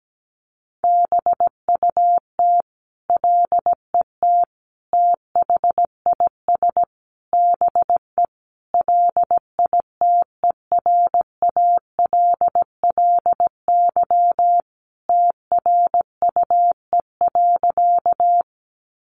Morse code, heard as a keyed tone, about 17 wpm, 700 Hz. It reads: BUT LET THIS BE LITERALLY TRUE.
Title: Sentences from Top 1400 Words 17wpm